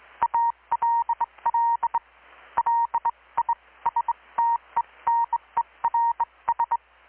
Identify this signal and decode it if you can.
Morse code, heard as a keyed tone, about 21 wpm, spaced as for 18 wpm, 930 Hz, with radio noise. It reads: ALL LISTENERS
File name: blog / swl / 6-abril-2019-12695-USB-KPS